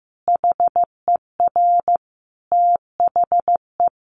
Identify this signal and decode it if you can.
Morse code, heard as a keyed tone, about 15 wpm, 700 Hz. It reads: HER THE
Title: Sentences from Top 800 Words 15wpm